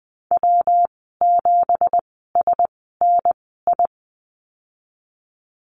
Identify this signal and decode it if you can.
Morse code, heard as a keyed tone, about 20 wpm, spaced as for 14 wpm, 700 Hz. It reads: W7SNI
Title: 167 Lesson 27g - Number 7 - Call Signs 20wpm